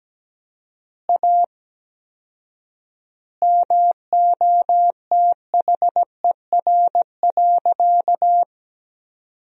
Morse code, heard as a keyed tone, about 17 wpm, 700 Hz. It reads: A MOTHER.
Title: 4x Word-Spacing - Sentences from Top 200 Words - 17wpm